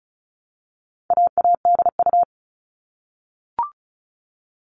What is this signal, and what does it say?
Morse code, heard as a keyed tone, about 34 wpm, 700 Hz. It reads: AADU
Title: All Cut Numbers - 4-Digit 35wpm